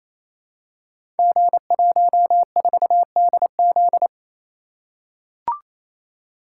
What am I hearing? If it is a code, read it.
Morse code, heard as a keyed tone, about 29 wpm, 700 Hz. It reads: G14DZ
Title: Code Groups - 5 Letter:Numbers - 28wpm